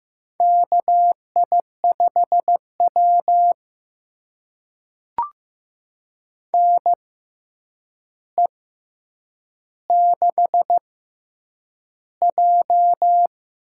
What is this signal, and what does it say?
Morse code, heard as a keyed tone, about 15 wpm, spaced as for 8 wpm, 700 Hz. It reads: KI5W NE6J
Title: USA Call Signs - 4 Characters - 6x Character Spacing - 15wpm